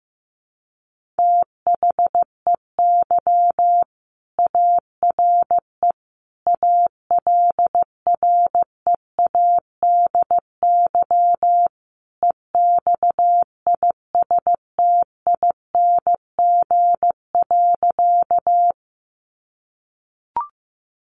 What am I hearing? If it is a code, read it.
Morse code, heard as a keyed tone, about 15 wpm, 700 Hz. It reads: THEY ARE ALREADY EXISTING.
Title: Sentences from Top 1500 Words 15wpm